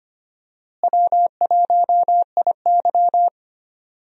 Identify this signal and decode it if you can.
Morse code, heard as a keyed tone, about 25 wpm, 700 Hz. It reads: W1IY
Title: USA Call Signs - 4 Characters - Repeated 2x - 25wpm